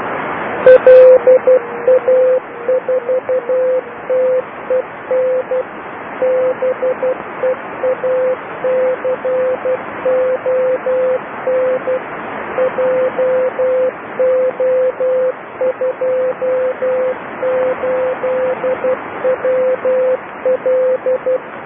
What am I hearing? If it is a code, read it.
Morse code, heard as a keyed tone, about 12 wpm, 510 Hz, with noise. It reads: LA4TEN BEACON JO28WL
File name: LA4TEN beacon 1008 28210 Doucy sito2